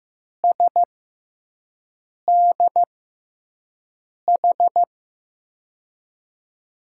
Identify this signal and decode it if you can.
Morse code, heard as a keyed tone, about 15 wpm, spaced as for 5 wpm, 700 Hz. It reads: SDH
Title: USA Call Signs - 6x Character Spacing - 15wpm